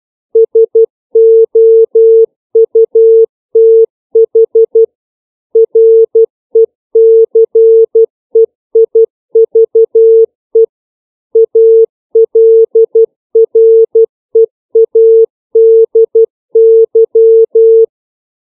Morse code, heard as a keyed tone, about 12 wpm, 450 Hz. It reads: SOUTH RECEIVE ALREADY
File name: CW / five common words